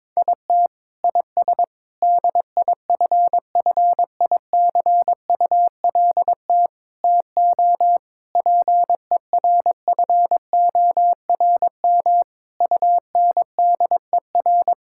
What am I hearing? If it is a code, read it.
Morse code, heard as a keyed tone, about 22 wpm, 700 Hz. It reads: IT IS DIFFICULT TO PERFORM UNDER
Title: Sentences from Top 1000 Words - Review Part 6 - 22wpm